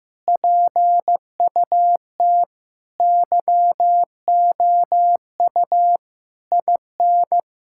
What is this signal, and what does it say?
Morse code, heard as a keyed tone, about 15 wpm, 700 Hz. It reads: PUT YOU IN
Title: Sentences from Top 1000 Words - Review Part 5 - 15wpm